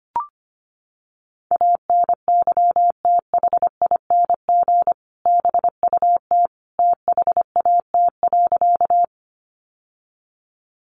Morse code, heard as a keyed tone, about 25 wpm, 700 Hz. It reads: ANYTHING BUT THAT.
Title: Sentences from Top 1000 Words 25wpm